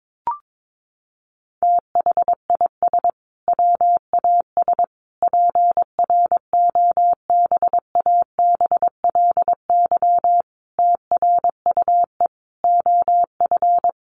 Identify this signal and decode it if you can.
Morse code, heard as a keyed tone, about 22 wpm, 700 Hz. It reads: THIS WAS PROBABLY TRUE OF
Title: Sentences from Top 700 Words 22wpm